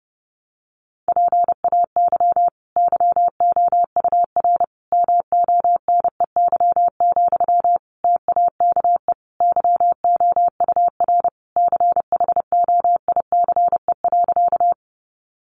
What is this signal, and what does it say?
Morse code, heard as a keyed tone, about 30 wpm, 700 Hz. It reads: PAY YOUR MONEY, TAKE YOUR CHOICE.
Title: Sentences from Top 900 Words 30wpm